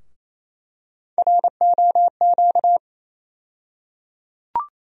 Morse code, heard as a keyed tone, about 28 wpm, 700 Hz. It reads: ROQ
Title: USA Call Sign Postfixes - 3 Letters 28wpm